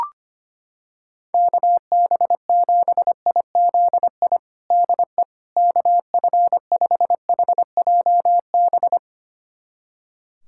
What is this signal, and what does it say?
Morse code, heard as a keyed tone, about 25 wpm, 700 Hz. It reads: KB7IZI DE KF5HJB
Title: USA Callsign DE Callsign - 6 Characters - 25wpm